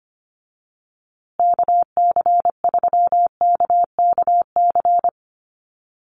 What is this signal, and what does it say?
Morse code, heard as a keyed tone, about 25 wpm, 700 Hz. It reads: KC3KKC
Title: USA Call Signs Repeated 3x 25wpm